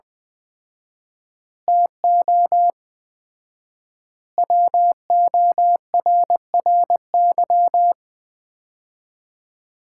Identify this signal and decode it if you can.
Morse code, heard as a keyed tone, about 21 wpm, 700 Hz. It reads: TO WORRY
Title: 4x Word-Spacing - Sets of 5 Words - 20wpm